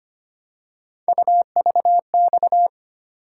Morse code, heard as a keyed tone, about 25 wpm, 700 Hz. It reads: UVX
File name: Three Letter-Number Combinations - 25wpm